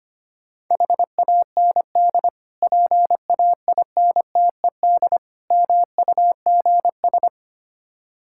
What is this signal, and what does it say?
Morse code, heard as a keyed tone, about 25 wpm, 700 Hz. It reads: HAND PAINTED MUGS
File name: practice / 25wpm / Difficult Words with Hints - 25wpm